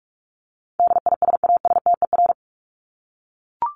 Morse code, heard as a keyed tone, about 44 wpm, 700 Hz. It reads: DISASTER
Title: Top 1501-1750 Words 45wpm